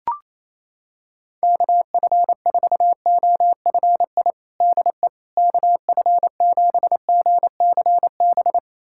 Morse code, heard as a keyed tone, about 28 wpm, 700 Hz. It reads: KF4OFI DE KF7GCB
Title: USA Callsign DE Callsign - 6 Characters - 28wpm